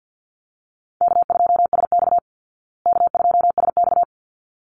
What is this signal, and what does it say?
Morse code, heard as a keyed tone, about 48 wpm, 700 Hz. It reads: K2SX K2SX
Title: Call Signs Repeated 3x - 50wpm